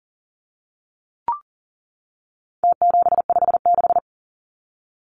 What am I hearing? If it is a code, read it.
Morse code, heard as a keyed tone, about 40 wpm, 700 Hz. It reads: T756
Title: Numbers - Contest Style - 4-Digit 40wpm